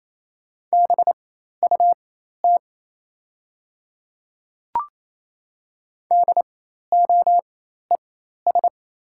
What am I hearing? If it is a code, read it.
Morse code, heard as a keyed tone, about 28 wpm, spaced as for 13 wpm, 700 Hz. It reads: BUT DOES
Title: 4x Character Spacing - Top 300 Words - 28wpm